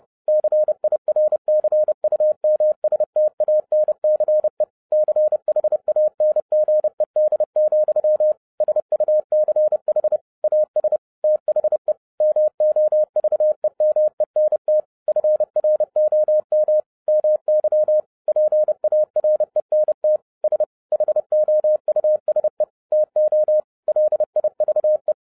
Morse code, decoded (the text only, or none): CIRCUMSTANCE CHANGED, SUCH AS THE MOVEMENT FROM MY PARENT S HOUSE TO LIVE